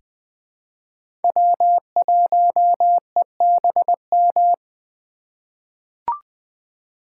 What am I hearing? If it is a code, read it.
Morse code, heard as a keyed tone, about 20 wpm, 700 Hz. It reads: W1EBM
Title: K1USN SST QSO Practice - 20wpm